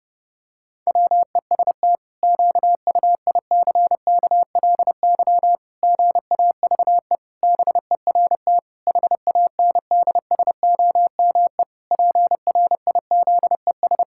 Morse code, heard as a keyed tone, about 30 wpm, 700 Hz. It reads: WEST QUICKLY GAVE BERT HANDSOME PRIZES
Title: Pangram Challenge v2 - 30wpm